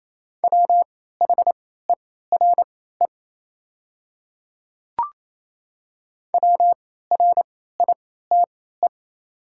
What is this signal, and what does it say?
Morse code, heard as a keyed tone, about 27 wpm, spaced as for 16 wpm, 700 Hz. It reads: WHERE WRITE
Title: 3x Character Spacing - Top 300 Words - 28wpm